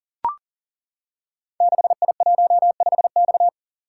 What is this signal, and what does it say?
Morse code, heard as a keyed tone, about 40 wpm, 690 Hz. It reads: 6I15X
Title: Code Groups - 5 Letter:Numbers - 40wpm